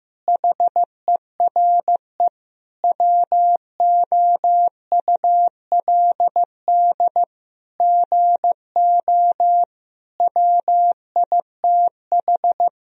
Morse code, heard as a keyed tone, about 15 wpm, 700 Hz. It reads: HERE WOULD GO WITH
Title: Sentences from Top 1000 Words - Review Part 3 - 15wpm